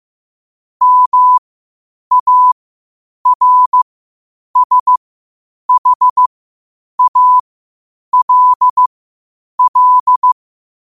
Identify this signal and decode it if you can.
Morse code, heard as a keyed tone, about 15 wpm, spaced as for 8 wpm, 1000 Hz. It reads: MARSHALL
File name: qotdfiles / 8wpm